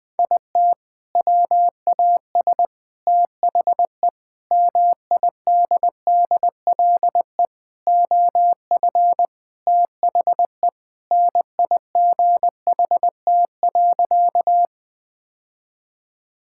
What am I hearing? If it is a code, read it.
Morse code, heard as a keyed tone, about 21 wpm, 700 Hz. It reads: IT WAS THE MIDDLE OF THE NIGHT.
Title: Sentences from Top 1000 Words - Review Part 6 - 20wpm